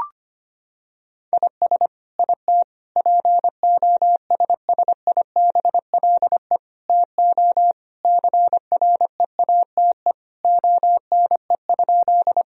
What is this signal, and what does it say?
Morse code, heard as a keyed tone, about 25 wpm, 700 Hz. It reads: IS IT POSSIBLE TO CREATE ONE?
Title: Sentences from Top 1000 Words - Review Part 3 - 25wpm